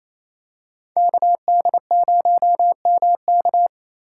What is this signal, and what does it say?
Morse code, heard as a keyed tone, about 28 wpm, 700 Hz. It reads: KD0MK
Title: USA Call Signs - 5 Characters - 4x Character Spacing - 28wpm